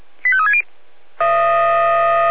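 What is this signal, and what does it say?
Five-tone selective call (ZVEI), heard as quick runs of five tones, tones of about 70 ms. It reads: 85360
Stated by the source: Received on an SDR; ZVEI five-tone call 85630.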